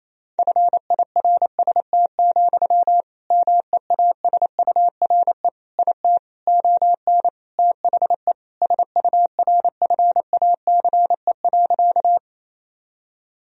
Morse code, heard as a keyed tone, about 28 wpm, 700 Hz. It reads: FIRST, MEASURE IT ON THE SURFACE.
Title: Sentences from Top 1000 Words - Review Part 3 - 28wpm